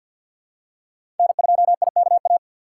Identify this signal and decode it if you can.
Morse code, heard as a keyed tone, about 51 wpm, 700 Hz. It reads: N2IKA
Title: USA Call Signs - 6x Character Spacing - 50wpm